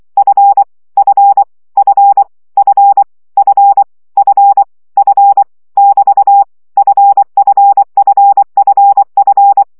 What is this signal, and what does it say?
Morse code, heard as a keyed tone, about 24 wpm, 800 Hz. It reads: F F F F F F F <BT> FFFFF
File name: CW gyakorlas f 20 baud